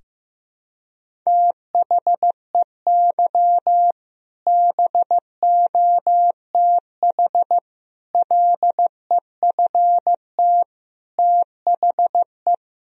Morse code, heard as a keyed tone, about 15 wpm, 700 Hz. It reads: THEY BOTH LEFT THE
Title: Sentences from Top 1000 Words - Review Part 1 - 15wpm